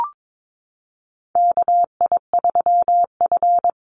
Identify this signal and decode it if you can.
Morse code, heard as a keyed tone, about 22 wpm, 700 Hz. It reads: KI3F
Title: Call Signs - 22wpm